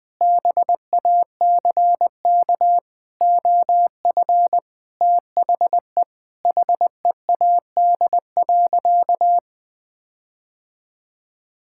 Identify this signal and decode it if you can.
Morse code, heard as a keyed tone, about 20 wpm, 700 Hz. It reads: BACK OF THE HEAD.
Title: Sentences with 6-Word Phrases - 20wpm